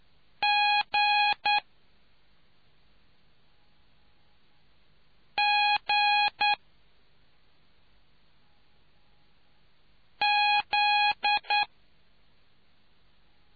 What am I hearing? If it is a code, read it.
Morse code, heard as a keyed tone, about 9 wpm, 830 Hz. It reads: G G Z